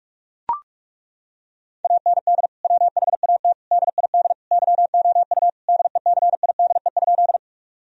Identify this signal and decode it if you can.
Morse code, heard as a keyed tone, about 41 wpm, 700 Hz. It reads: AND WHAT DID YOU DECIDE?
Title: Sentences from Top 1000 Words - Review Part 3 - 45wpm